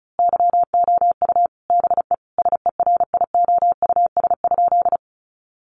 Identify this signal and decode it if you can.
Morse code, heard as a keyed tone, about 34 wpm, 700 Hz. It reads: YOU BE SERIOUS?
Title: Sentences from Top 1100 Words 35wpm